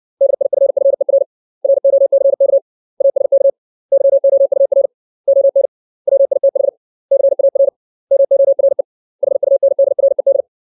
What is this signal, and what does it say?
Morse code, heard as a keyed tone, about 60 wpm, 550 Hz. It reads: DIFFER ROCK ASK YOUR ON WITH CAR MORE HUNDRED
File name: 632 words / 632 Words 60WPM